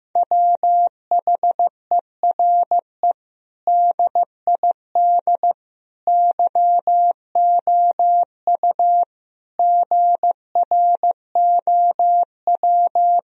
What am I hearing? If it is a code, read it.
Morse code, heard as a keyed tone, about 15 wpm, 700 Hz. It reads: WHERE DID YOU GROW
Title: Sentences from Top 1000 Words - Review Part 1 - 15wpm